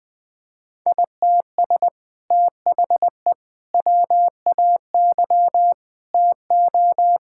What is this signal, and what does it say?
Morse code, heard as a keyed tone, about 21 wpm, 700 Hz. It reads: ITS THE WAY TO
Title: Sentences with 2-Word Phrases - 20wpm